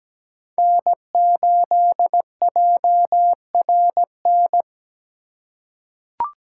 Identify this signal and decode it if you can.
Morse code, heard as a keyed tone, about 17 wpm, 700 Hz. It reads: N8JRN
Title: USA Call Signs - 5 Characters - 4x Character Spacing - 17wpm